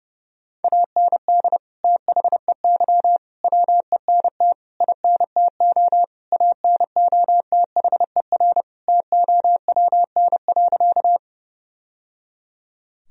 Morse code, heard as a keyed tone, about 30 wpm, 700 Hz. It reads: AND THEY WENT INTO ANOTHER TOWN.
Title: Sentences from Top 1000 Words - Review Part 2 - 30wpm